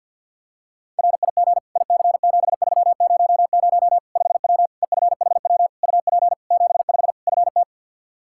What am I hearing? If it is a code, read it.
Morse code, heard as a keyed tone, about 51 wpm, 700 Hz. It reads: RIG IC7300 5W EFHW UP 75 FT
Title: Practice QSO - Exchange 2 - Info - 50wpm